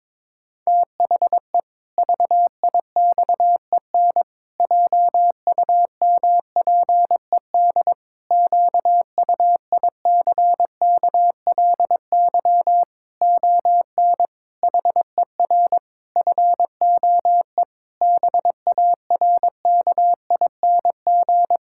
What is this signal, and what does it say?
Morse code, heard as a keyed tone, about 22 wpm, 700 Hz. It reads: THE VIXEN JUMPED QUICKLY ON HER FOE BARKING
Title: Pangram Challenge v2 - 22wpm